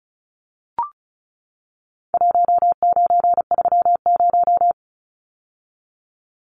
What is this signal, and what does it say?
Morse code, heard as a keyed tone, about 34 wpm, 700 Hz. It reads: 1930
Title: Numbers - Contest Style - 4-Digit 35wpm